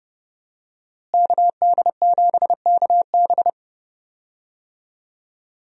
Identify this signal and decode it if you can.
Morse code, heard as a keyed tone, about 32 wpm, 700 Hz. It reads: KD7KB
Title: USA Call Signs - 5 Characters - Repeated 2x - 30wpm